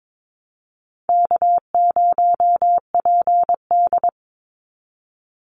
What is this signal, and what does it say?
Morse code, heard as a keyed tone, about 22 wpm, 700 Hz. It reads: K0PD
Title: Easy Call Signs - 22wpm